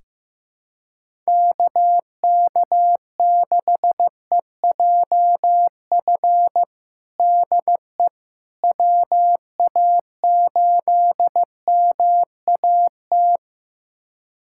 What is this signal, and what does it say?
Morse code, heard as a keyed tone, about 15 wpm, 700 Hz. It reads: KK6EJF DE WA8MAT
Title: USA Callsign DE Callsign 15wpm